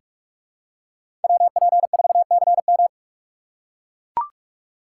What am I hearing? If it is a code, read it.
Morse code, heard as a keyed tone, about 41 wpm, 700 Hz. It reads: WP4CM
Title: USA Call Signs - 3x Character Spacing - 45wpm